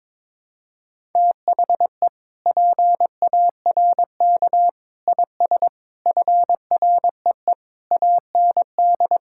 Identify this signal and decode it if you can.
Morse code, heard as a keyed tone, about 22 wpm, 700 Hz. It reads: THE PARK IS FREE AND
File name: Sentences with 7-Word Phrases - 22wpm